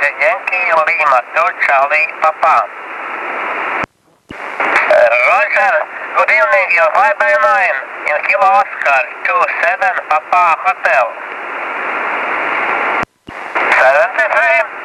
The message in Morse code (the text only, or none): none